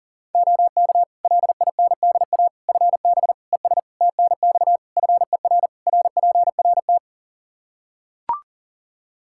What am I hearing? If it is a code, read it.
Morse code, heard as a keyed tone, about 40 wpm, 700 Hz. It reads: OK LINDA FB ES TNX FER RPRT
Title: Practice QSO - Exchange 2 - Info - 40wpm